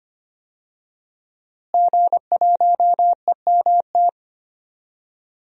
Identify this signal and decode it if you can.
Morse code, heard as a keyed tone, about 25 wpm, 700 Hz. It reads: G1EMT
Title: DXCC-100 Call Signs - 5 Characters - Repeated 2x - 25wpm